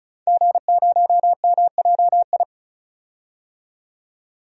Morse code, heard as a keyed tone, about 35 wpm, 700 Hz. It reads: G0MJI
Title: DXCC-100 Call Signs - 5 Characters - 35wpm